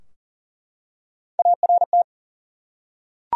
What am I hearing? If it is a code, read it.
Morse code, heard as a keyed tone, about 40 wpm, 700 Hz. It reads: ART
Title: USA Common Names - Max 3 Letters - 40wpm